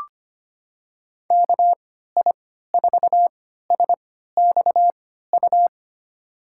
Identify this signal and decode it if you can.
Morse code, heard as a keyed tone, about 25 wpm, spaced as for 14 wpm, 700 Hz. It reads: KI4SXU
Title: USA Call Signs - 6 Characters - 3x Character Spacing - 25wpm